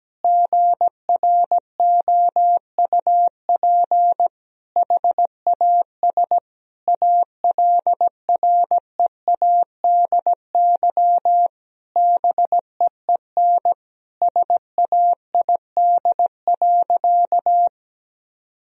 Morse code, heard as a keyed tone, about 17 wpm, 700 Hz. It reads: GROUP HAS ALREADY BEEN SAID.
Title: Sentences from Top 1000 Words - Review Part 1 - 17wpm